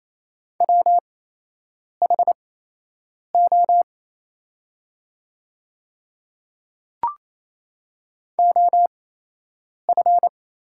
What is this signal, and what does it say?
Morse code, heard as a keyed tone, about 27 wpm, spaced as for 8 wpm, 690 Hz. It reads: WHO OF
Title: 8x Character Spacing - Top 200 Words - 28wpm